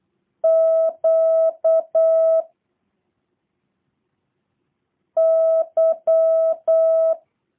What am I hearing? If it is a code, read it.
Morse code, heard as a keyed tone, about 8 wpm, 650 Hz. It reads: Q Y
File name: jh8cbh / 2023cw / review1-6